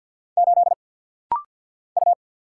Morse code, heard as a keyed tone, about 50 wpm, 700 Hz. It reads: 8 U
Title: Single Letter-Number - Rapid-Fire 50wpm